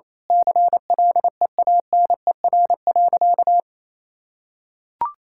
Morse code, heard as a keyed tone, about 27 wpm, 700 Hz. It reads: CLEANER.
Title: Pangram Challenge v2 - 28wpm